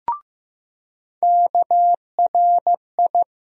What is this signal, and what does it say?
Morse code, heard as a keyed tone, about 15 wpm, 700 Hz. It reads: KRI
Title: USA Call Sign Postfixes - 3 Letters 15wpm